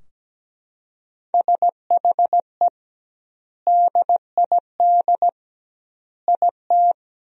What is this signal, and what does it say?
Morse code, heard as a keyed tone, about 17 wpm, 700 Hz. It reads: SHE DID IT
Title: 2x Word-Spacing - Sentences from Top 400 Words - 17wpm